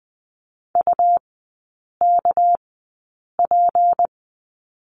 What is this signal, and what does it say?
Morse code, heard as a keyed tone, about 20 wpm, spaced as for 8 wpm, 690 Hz. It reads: UKP
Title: 198 Lesson 32e - Letter k - Three Characters 20wpm